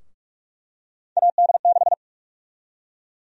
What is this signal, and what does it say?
Morse code, heard as a keyed tone, about 46 wpm, 700 Hz. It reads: AD6
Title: USA Call Sign Prefixes - Repeated 2x - 45wpm